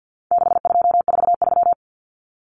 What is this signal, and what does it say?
Morse code, heard as a keyed tone, about 50 wpm, 700 Hz. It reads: 6243
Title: Numbers - 4-Digit 50wpm